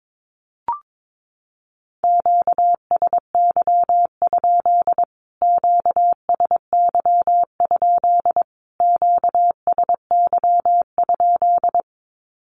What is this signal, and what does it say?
Morse code, heard as a keyed tone, about 22 wpm, 700 Hz. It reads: QSY? QSY? QSY?
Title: Q-Codes Repeated 3x 22wpm